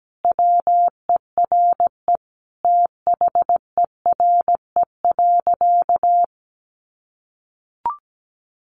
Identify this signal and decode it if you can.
Morse code, heard as a keyed tone, about 17 wpm, 700 Hz. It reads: WERE THERE.
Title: Sentences from Top 1000 Words 17wpm